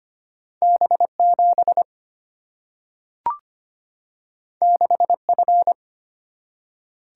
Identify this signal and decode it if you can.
Morse code, heard as a keyed tone, about 25 wpm, 700 Hz. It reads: B7 6F
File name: Two Letter-Number Combinations - 25wpm